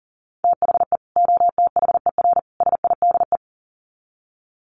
Morse code, heard as a keyed tone, about 40 wpm, 700 Hz. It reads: THE OTHER SIDE
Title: Sets of 3 Words 40wpm